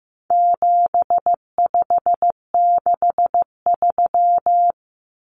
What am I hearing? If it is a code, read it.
Morse code, heard as a keyed tone, about 15 wpm, 700 Hz. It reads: 7563
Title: Numbers - 4-Digit 15wpm